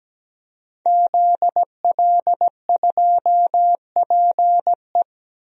Morse code, heard as a keyed tone, about 17 wpm, 700 Hz. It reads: ZL2PE
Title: DXCC-100 Call Signs - 5 Characters - Repeated 3x - 17wpm